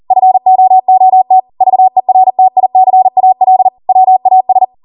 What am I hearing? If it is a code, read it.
Morse code, heard as a keyed tone, about 40 wpm, 750 Hz. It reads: FOOT VERTICAL WAS